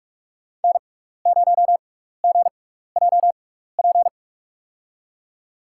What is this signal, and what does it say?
Morse code, heard as a keyed tone, about 47 wpm, spaced as for 16 wpm, 700 Hz. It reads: N0GJP
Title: USA Call Signs - 5 Characters - 6x Character Spacing - 45wpm